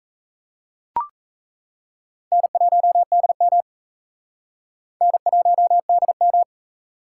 Morse code, heard as a keyed tone, about 37 wpm, 700 Hz. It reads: N1DM N1DM